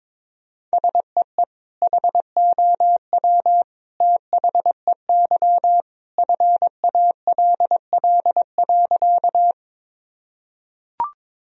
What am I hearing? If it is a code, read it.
Morse code, heard as a keyed tone, about 22 wpm, 700 Hz. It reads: SEE HOW THEY FALL.
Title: Sentences from Top 1000 Words - Review Part 2 - 22wpm